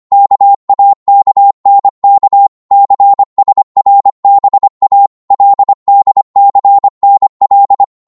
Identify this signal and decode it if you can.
Morse code, heard as a keyed tone, about 25 wpm, 800 Hz. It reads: KAKNK CSRBA LDCNL
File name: Morse / ltr-25wpm